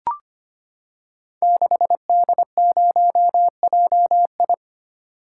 Morse code, heard as a keyed tone, about 25 wpm, 690 Hz. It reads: 6D0JI
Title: Code Groups - 5 Letter:Numbers - 25wpm